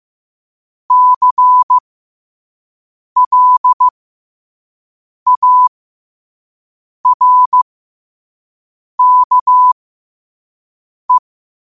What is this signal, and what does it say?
Morse code, heard as a keyed tone, about 15 wpm, spaced as for 5 wpm, 1000 Hz. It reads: CLARKE